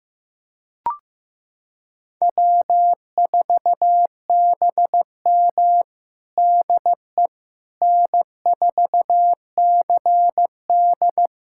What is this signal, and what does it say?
Morse code, heard as a keyed tone, about 15 wpm, 700 Hz. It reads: W4BM DE N4CD
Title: USA Callsign DE Callsign - 4 Characters - 15wpm